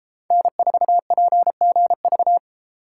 Morse code, heard as a keyed tone, about 33 wpm, 700 Hz. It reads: N4PGV